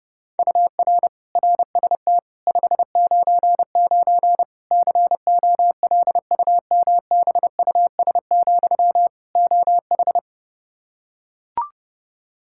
Morse code, heard as a keyed tone, about 30 wpm, 700 Hz. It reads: UR RST 599 COLUMBUS, OH
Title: K1USN SST QSO Practice 30wpm